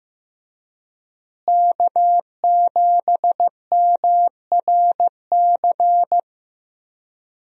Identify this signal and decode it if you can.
Morse code, heard as a keyed tone, about 15 wpm, 700 Hz. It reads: K7MRC
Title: USA Call Signs - 5 Characters - Repeated 2x - 15wpm